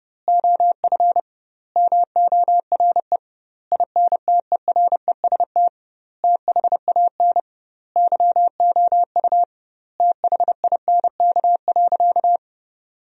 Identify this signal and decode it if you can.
Morse code, heard as a keyed tone, about 30 wpm, 700 Hz. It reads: OF MORE INTEREST THAN YOU THINK.
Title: 2x Word-Spacing - Sentences from Top 400 Words - 30wpm